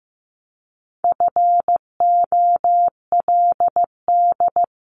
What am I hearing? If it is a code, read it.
Morse code, heard as a keyed tone, about 15 wpm, 700 Hz. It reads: FOLD
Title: Top 1751-2000 Words 15wpm